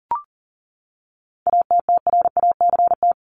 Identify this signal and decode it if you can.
Morse code, heard as a keyed tone, about 40 wpm, 700 Hz. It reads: ATTRACT
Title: Top 1501-1750 Words 40wpm